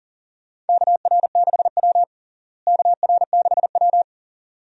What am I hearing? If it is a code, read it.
Morse code, heard as a keyed tone, about 40 wpm, 700 Hz. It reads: KR6W KR6W
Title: USA Call Signs - 4 Characters - Repeated 2x - 40wpm